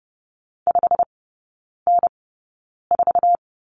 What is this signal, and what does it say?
Morse code, heard as a keyed tone, about 30 wpm, spaced as for 9 wpm, 700 Hz. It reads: 5N4
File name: Lesson 17e - Letter u - Three Characters 30wpm